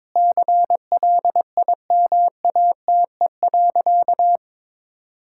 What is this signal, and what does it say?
Morse code, heard as a keyed tone, about 22 wpm, 700 Hz. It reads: CLIMATE.
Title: Sentences from Top 1600 Words 22wpm